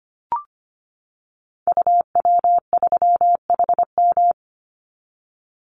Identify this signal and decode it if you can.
Morse code, heard as a keyed tone, about 25 wpm, 700 Hz. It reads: UW3HM
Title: Call Signs - 25wpm